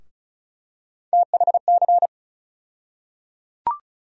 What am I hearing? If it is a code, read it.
Morse code, heard as a keyed tone, about 36 wpm, 700 Hz. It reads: THC
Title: USA Call Sign Postfixes - 3 Letters 35wpm